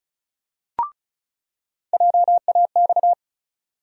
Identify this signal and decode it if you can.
Morse code, heard as a keyed tone, about 36 wpm, 700 Hz. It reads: JAX
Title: USA Call Sign Postfixes - 3 Letters 35wpm